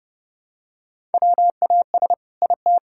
Morse code, heard as a keyed tone, about 30 wpm, 700 Hz. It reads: WAS IT
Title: Sentences from Top 1000 Words - Review Part 5 - 30wpm